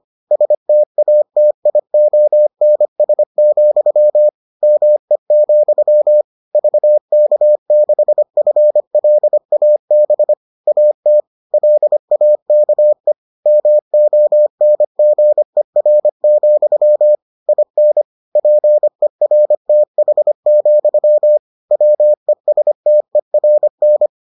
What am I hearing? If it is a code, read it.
Morse code, heard as a keyed tone, about 25 wpm, 600 Hz. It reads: STATIONS, ME, VK6FLAB AT LAKE MONGER, IN PERTH, WESTERN